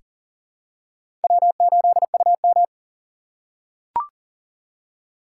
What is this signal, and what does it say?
Morse code, heard as a keyed tone, about 37 wpm, 700 Hz. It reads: W8UM
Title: USA Call Signs - 4 Characters - 3x Character Spacing - 40wpm